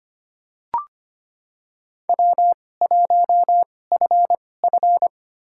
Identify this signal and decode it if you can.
Morse code, heard as a keyed tone, about 25 wpm, spaced as for 18 wpm, 700 Hz. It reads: W1FF
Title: USA Call Signs - 4 Characters - 2x Character Spacing - 25wpm